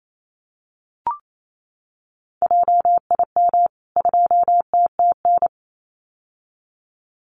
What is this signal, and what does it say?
Morse code, heard as a keyed tone, about 28 wpm, 700 Hz. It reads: JIM 2TTN
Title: CWT Exchange 28wpm